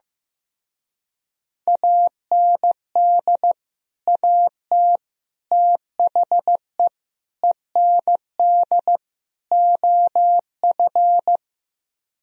Morse code, heard as a keyed tone, about 15 wpm, 700 Hz. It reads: AND AT THE END OF
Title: Speed-Racing - Sets of 5 Words 15wpm